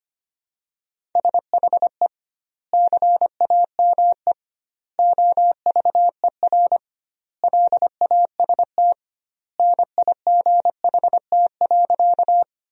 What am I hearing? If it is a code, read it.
Morse code, heard as a keyed tone, about 25 wpm, 700 Hz. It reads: SHE CAME OVER LAST NIGHT.
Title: 2x Word-Spacing - Sentences from Top 300 Words - 25wpm